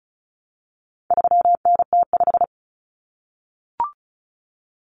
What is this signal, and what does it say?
Morse code, heard as a keyed tone, about 34 wpm, 700 Hz. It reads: 3NT5
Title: Common Cut Numbers - 4-Digit 35wpm